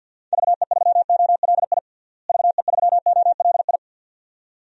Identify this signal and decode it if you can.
Morse code, heard as a keyed tone, about 50 wpm, 700 Hz. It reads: VE3OLI VE3OLI
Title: DXCC-100 Call Signs - 6 Characters - Repeated 2x - 50wpm